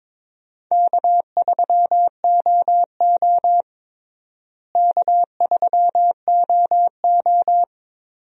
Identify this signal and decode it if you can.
Morse code, heard as a keyed tone, about 22 wpm, 700 Hz. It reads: K3OO K3OO
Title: USA Call Signs - 4 Characters - Repeated 3x - 22wpm